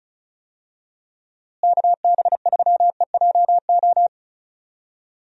Morse code, heard as a keyed tone, about 36 wpm, 700 Hz. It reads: KB3EJO
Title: USA Call Signs - 6 Characters - Repeated 3x - 35wpm